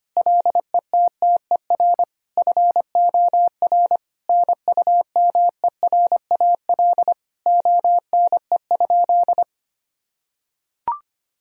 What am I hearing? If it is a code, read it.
Morse code, heard as a keyed tone, about 25 wpm, 700 Hz. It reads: LETTER FOR NUMERAL ONE?
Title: Sentences from Top 1000 Words - Review Part 2 - 25wpm